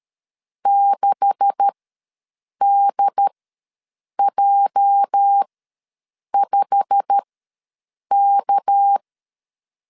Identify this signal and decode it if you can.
Morse code, heard as a keyed tone, about 13 wpm, spaced as for 7 wpm, 800 Hz. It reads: 6DJ5K
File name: LEZIONI / 10 SIGNS / 10A SLASH ws 12wpm 3443